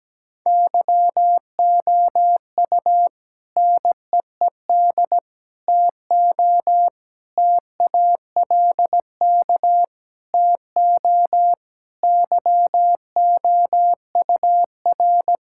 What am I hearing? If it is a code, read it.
Morse code, heard as a keyed tone, about 17 wpm, 700 Hz. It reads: YOU NEED TO TALK TO YOUR
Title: Sentences from Top 1600 Words 17wpm